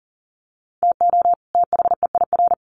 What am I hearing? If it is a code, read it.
Morse code, heard as a keyed tone, about 40 wpm, 700 Hz. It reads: TO THEIR
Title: Sets of 2 Words 40wpm